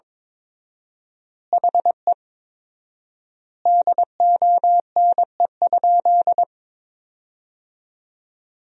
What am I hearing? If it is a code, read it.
Morse code, heard as a keyed tone, about 22 wpm, 700 Hz. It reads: HE DONE?
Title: 4x Word-Spacing - Sentences from Top 500 Words - 22wpm